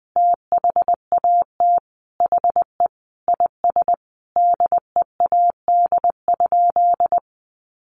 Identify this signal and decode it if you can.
Morse code, heard as a keyed tone, about 20 wpm, 700 Hz. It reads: THAT HE IS DEAD?
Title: Sentences from Top 1500 Words 20wpm